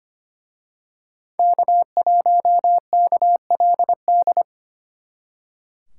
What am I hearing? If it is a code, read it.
Morse code, heard as a keyed tone, about 31 wpm, 700 Hz. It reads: K1KLD
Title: Speed Racing - USA 1x3 Call Signs 25wpm